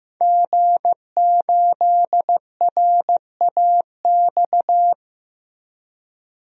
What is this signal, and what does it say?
Morse code, heard as a keyed tone, about 15 wpm, 700 Hz. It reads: G8RAX
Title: DXCC-100 Call Signs - 5 Characters - 15wpm